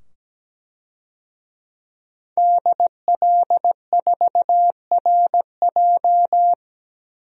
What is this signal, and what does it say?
Morse code, heard as a keyed tone, about 17 wpm, 700 Hz. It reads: DL4RJ
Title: DXCC-100 Call Signs - 5 Characters - 17wpm